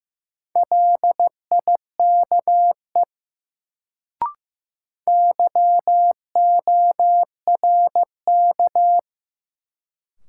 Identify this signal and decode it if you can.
Morse code, heard as a keyed tone, about 15 wpm, 700 Hz. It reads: LIKE YORK
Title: Sets of 2 Words - New Word Drill - 15wpm